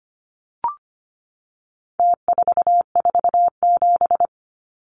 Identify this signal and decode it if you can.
Morse code, heard as a keyed tone, about 25 wpm, 700 Hz. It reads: T447
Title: Numbers - Contest Style - 4-Digit 25wpm